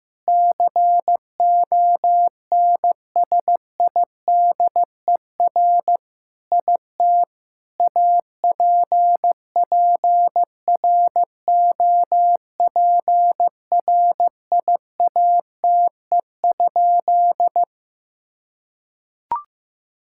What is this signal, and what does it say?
Morse code, heard as a keyed tone, about 15 wpm, 700 Hz. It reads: CONSIDER IT APPROPRIATE?
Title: Sentences from Top 1000 Words - Review Part 5 - 15wpm